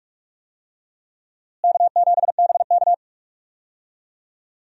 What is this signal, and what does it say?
Morse code, heard as a keyed tone, about 46 wpm, 700 Hz. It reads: K7BK
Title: USA Call Signs - 4 Characters - Repeated 3x - 45wpm